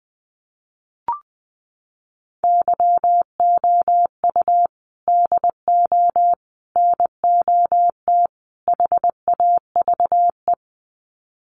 Morse code, heard as a keyed tone, about 20 wpm, 700 Hz. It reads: YOU DO NOT HAVE
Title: Sets of 4 Words 20wpm